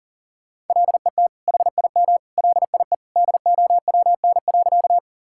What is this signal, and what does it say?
Morse code, heard as a keyed tone, about 37 wpm, 700 Hz. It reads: LET HIM LIE DOWN.
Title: Sentences from Top 1600 Words 40wpm